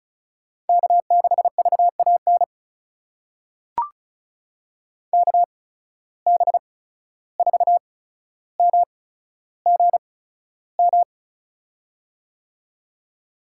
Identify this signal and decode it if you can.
Morse code, heard as a keyed tone, about 36 wpm, spaced as for 14 wpm, 700 Hz. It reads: K6VAN KB4MGM
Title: USA Call Signs - 8x Character Spacing - 35wpm